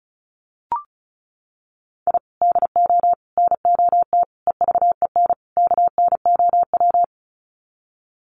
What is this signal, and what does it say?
Morse code, heard as a keyed tone, about 34 wpm, 700 Hz. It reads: I DO NOT EVEN KNOW
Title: Sets of 5 Words 35wpm